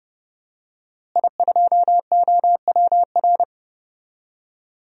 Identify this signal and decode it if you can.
Morse code, heard as a keyed tone, about 30 wpm, 700 Hz. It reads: I2OWR
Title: DXCC-100 Call Signs - 5 Characters - Repeated 2x - 30wpm